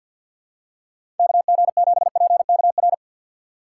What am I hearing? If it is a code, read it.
Morse code, heard as a keyed tone, about 52 wpm, 700 Hz. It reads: KG7PKR
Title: USA Call Signs - 6 Characters - 4x Character Spacing - 50wpm